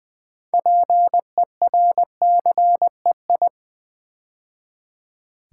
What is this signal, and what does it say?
Morse code, heard as a keyed tone, about 20 wpm, 700 Hz. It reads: PERCEI
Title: Build A Word - 3 Letters At A Time - 8-Letter Words - 20wpm